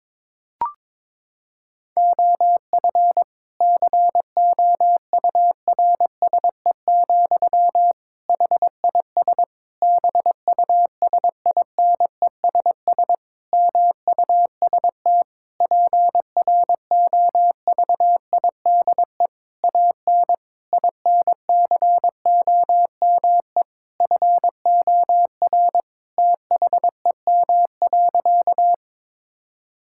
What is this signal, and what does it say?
Morse code, heard as a keyed tone, about 22 wpm, 700 Hz. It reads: OF COURSE, HIS BUSINESS MUST PROVIDE AN INCOME FOR THEM.
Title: Sentences from Top 1000 Words - Review Part 5 - 22wpm